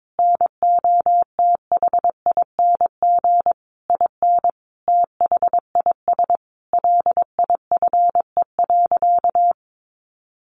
Morse code, heard as a keyed tone, about 22 wpm, 700 Hz. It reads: NOTHING IN THIS LIFE.
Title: Sentences from Top 1200 Words 22wpm